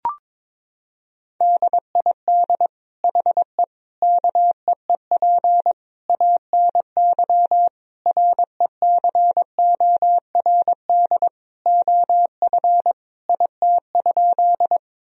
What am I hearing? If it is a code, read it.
Morse code, heard as a keyed tone, about 22 wpm, 700 Hz. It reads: DID HE KEEP ANY RECORD OF IT?
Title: Sentences from Top 1000 Words - Review Part 3 - 22wpm